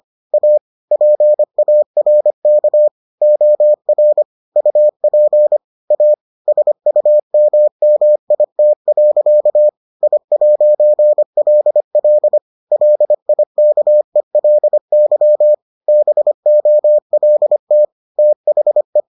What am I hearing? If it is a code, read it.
Morse code, heard as a keyed tone, about 25 wpm, 600 Hz. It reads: A PARK OR UP A SUMMIT. I'LL LIKELY BOLT THE